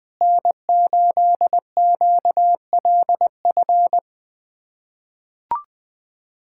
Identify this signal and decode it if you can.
Morse code, heard as a keyed tone, about 20 wpm, 700 Hz. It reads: N8QLF
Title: USA Callsign DE Callsign - 5 Characters - 20wpm